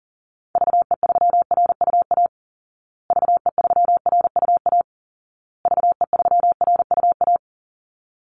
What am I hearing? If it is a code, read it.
Morse code, heard as a keyed tone, about 40 wpm, 700 Hz. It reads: VE3RUA VE3RUA VE3RUA
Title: Call Signs Repeated 2x - 40wpm